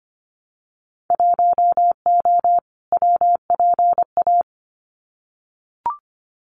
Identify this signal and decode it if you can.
Morse code, heard as a keyed tone, about 25 wpm, 700 Hz. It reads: 1O WPA
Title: Winter Field Day Exchanges 25wpm